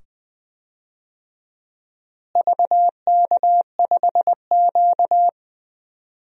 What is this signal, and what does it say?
Morse code, heard as a keyed tone, about 20 wpm, 700 Hz. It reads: VK5Q
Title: DXCC-100 Call Signs - 4 Characters - Repeated 2x - 20wpm